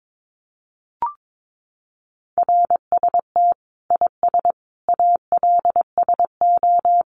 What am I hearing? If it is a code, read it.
Morse code, heard as a keyed tone, about 22 wpm, 700 Hz. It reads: RST IS ALSO
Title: Common QSO Phrases 22wpm